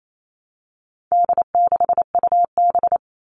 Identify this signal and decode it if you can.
Morse code, heard as a keyed tone, about 28 wpm, 700 Hz. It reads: D6UB
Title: All Cut Numbers - 4-Digit 28wpm